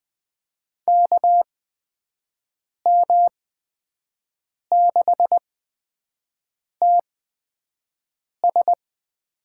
Morse code, heard as a keyed tone, about 20 wpm, spaced as for 5 wpm, 700 Hz. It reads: KM6TS
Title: USA Call Signs - 6 Characters - 8x Character Spacing - 20wpm